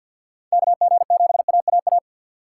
Morse code, heard as a keyed tone, about 50 wpm, 700 Hz. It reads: KG7AAA
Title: USA Call Signs - 6 Characters - 3x Character Spacing - 50wpm